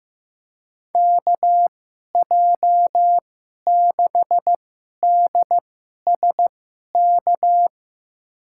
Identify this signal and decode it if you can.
Morse code, heard as a keyed tone, about 15 wpm, spaced as for 11 wpm, 700 Hz. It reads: KJ6DSK
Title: USA Call Signs - 6 Characters - 2x Character Spacing - 15wpm